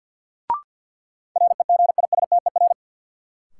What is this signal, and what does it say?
Morse code, heard as a keyed tone, about 53 wpm, 700 Hz. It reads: REGISTER
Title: Sentences from Top 1200 Words - New Word Drill - 50wpm